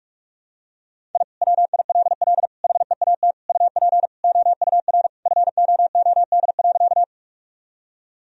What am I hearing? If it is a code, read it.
Morse code, heard as a keyed tone, about 43 wpm, 700 Hz. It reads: I WILL HEAT UP OUR FOOD.
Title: Sentences from Top 1000 Words - Review Part 3 - 45wpm